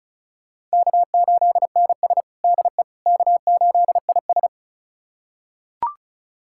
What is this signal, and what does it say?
Morse code, heard as a keyed tone, about 36 wpm, 700 Hz. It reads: K8NS DE K8IS
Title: USA Callsign DE Callsign - 4 Characters - 35wpm